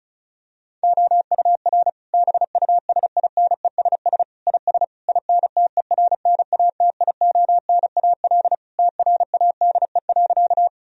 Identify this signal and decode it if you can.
Morse code, heard as a keyed tone, about 36 wpm, 700 Hz. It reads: OUR BUSINESS IS INTERNATIONAL TRADE.
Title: Sentences from Top 1000 Words - Review Part 4 - 35wpm